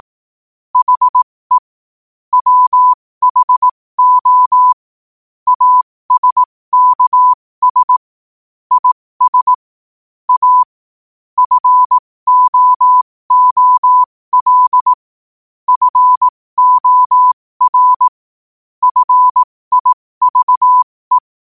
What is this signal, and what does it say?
Morse code, heard as a keyed tone, about 18 wpm, spaced as for 15 wpm, 1000 Hz. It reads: HE WHO ASKS IS A FOOL FOR FIVE
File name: qotdfiles / 15wpm